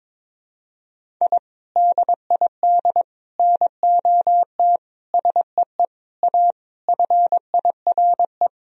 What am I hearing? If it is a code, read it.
Morse code, heard as a keyed tone, about 22 wpm, 700 Hz. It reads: I DID NOT SEE A FIRE
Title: Sentences from Top 1000 Words - Review Part 2 - 22wpm